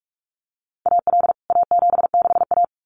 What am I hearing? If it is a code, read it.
Morse code, heard as a keyed tone, about 44 wpm, 700 Hz. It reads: AL A76A
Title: CWT Exchange 45wpm